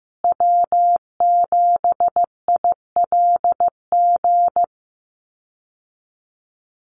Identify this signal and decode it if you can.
Morse code, heard as a keyed tone, about 15 wpm, 700 Hz. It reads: W7ILG
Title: USA Call Signs Repeated 3x 15wpm